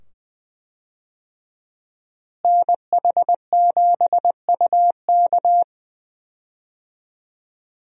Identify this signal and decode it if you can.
Morse code, heard as a keyed tone, about 21 wpm, 700 Hz. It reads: NH7UK
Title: USA Call Signs - 5 Characters - Repeated 2x - 20wpm